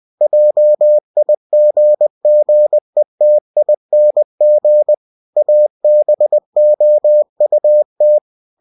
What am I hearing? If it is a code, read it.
Morse code, heard as a keyed tone, about 20 wpm, 600 Hz. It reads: JIGGETING ABOUT
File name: cw-books / winnie-the-pooh / winnie-20wpm-page-0040